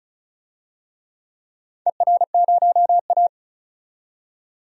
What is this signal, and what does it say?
Morse code, heard as a keyed tone, about 34 wpm, 700 Hz. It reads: ER0A
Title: DXCC-100 Call Signs - 4 Characters - 35wpm